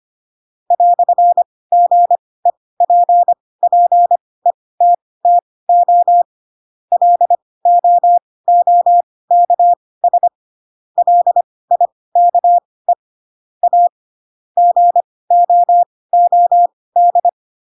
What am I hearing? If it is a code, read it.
Morse code, heard as a keyed tone, about 25 wpm, spaced as for 18 wpm, 700 Hz. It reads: "GEPPETTO LOOKS LIKE A GOOD
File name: Pinocchio ch3 12 25 ik1hge